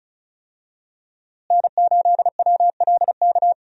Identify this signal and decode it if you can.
Morse code, heard as a keyed tone, about 34 wpm, 700 Hz. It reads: N8WLK
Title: USA Call Signs - 5 Characters - Repeated 3x - 35wpm